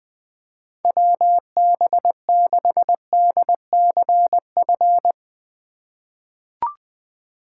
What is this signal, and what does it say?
Morse code, heard as a keyed tone, about 20 wpm, 700 Hz. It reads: WB6DCF
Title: USA Call Signs - 6 Characters - 4x Character Spacing - 20wpm